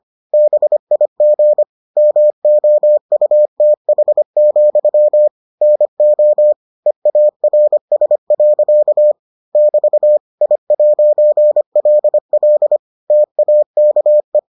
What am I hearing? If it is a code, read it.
Morse code, heard as a keyed tone, about 25 wpm, 600 Hz. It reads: BIG MOUTH, NO EARS. <BT> I'LL TAKE